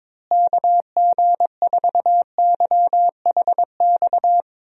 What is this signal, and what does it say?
Morse code, heard as a keyed tone, about 22 wpm, 700 Hz. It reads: KG4YHX
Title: USA Call Signs - 6 Characters - Repeated 3x - 22wpm